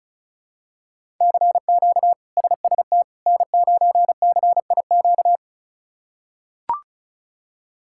Morse code, heard as a keyed tone, about 34 wpm, 700 Hz. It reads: CQ SST N9CIQ
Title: K1USN SST QSO Practice - 35wpm